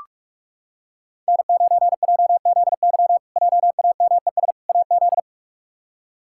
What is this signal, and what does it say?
Morse code, heard as a keyed tone, about 47 wpm, 700 Hz. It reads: N9JZY JAMES AZ
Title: K1USN SST QSO Practice - 45wpm